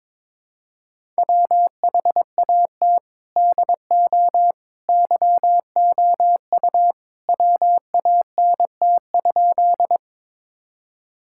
Morse code, heard as a keyed tone, about 22 wpm, 700 Hz. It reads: WHAT DO YOU WANT?
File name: practice / 22wpm / Sentences with 2-Word Phrases - 22wpm